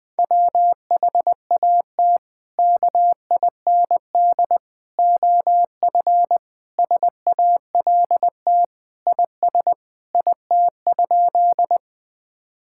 Morse code, heard as a keyed tone, about 20 wpm, 700 Hz. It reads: WHAT KIND OF SALT IS IT?
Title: Sentences from Top 1000 Words - Review Part 5 - 20wpm